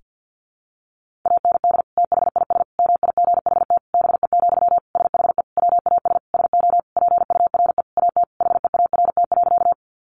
Morse code, heard as a keyed tone, about 48 wpm, 700 Hz. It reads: AND THIS MIGHT BE, SHE WAS SO PURE AT HEART.
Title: Sentences from Top 1500 Words 50wpm